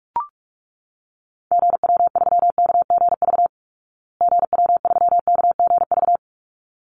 Morse code, heard as a keyed tone, about 44 wpm, 700 Hz. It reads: GW3KGV GW3KGV
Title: Call Signs Repeated 3x - 45wpm